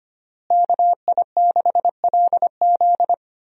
Morse code, heard as a keyed tone, about 25 wpm, 700 Hz. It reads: KI6LZ
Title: USA Call Signs - 5 Characters - Repeated 3x - 25wpm